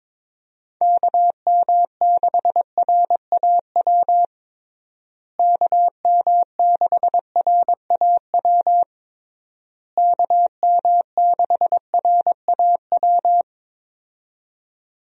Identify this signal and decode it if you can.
Morse code, heard as a keyed tone, about 22 wpm, 700 Hz. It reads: KM6RAW KM6RAW KM6RAW
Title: USA Call Signs - 6 Characters - Repeated 3x - 22wpm